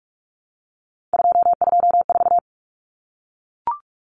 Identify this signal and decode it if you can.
Morse code, heard as a keyed tone, about 44 wpm, 700 Hz. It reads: 224
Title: Common Cut Numbers - 3-Digit 45wpm